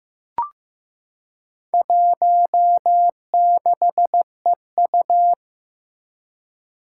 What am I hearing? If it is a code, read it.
Morse code, heard as a keyed tone, about 15 wpm, 700 Hz. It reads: 16EU
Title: Four Challenging Letter-Number Combinations 15wpm